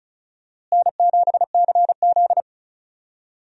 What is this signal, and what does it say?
Morse code, heard as a keyed tone, about 36 wpm, 700 Hz. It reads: N7CZ
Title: USA Call Signs - 4 Characters - Repeated 3x - 35wpm